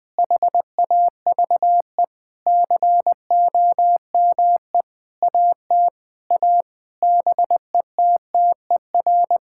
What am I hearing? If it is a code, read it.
Morse code, heard as a keyed tone, about 20 wpm, 700 Hz. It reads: HAVE COME AT A BETTER
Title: Sentences with 8-Word Phrases - 20wpm